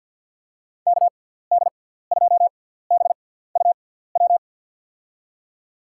Morse code, heard as a keyed tone, about 52 wpm, spaced as for 17 wpm, 700 Hz. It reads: KD2BUW
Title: USA Call Signs - 6 Characters - 6x Character Spacing - 50wpm